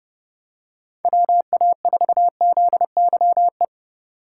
Morse code, heard as a keyed tone, about 30 wpm, 700 Hz. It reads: WA4ZYE
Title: USA Call Signs - 3x Character Spacing - 30wpm